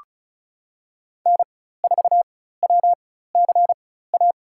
Morse code, heard as a keyed tone, about 34 wpm, spaced as for 16 wpm, 700 Hz. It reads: N4WCA
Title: USA Call Signs - 5 Characters - 4x Character Spacing - 35wpm